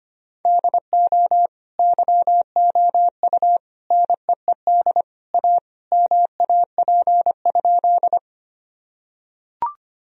Morse code, heard as a keyed tone, about 25 wpm, 700 Hz. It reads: DO YOU NEED A MAP?
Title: Sentences from Top 1000 Words - Review Part 2 - 25wpm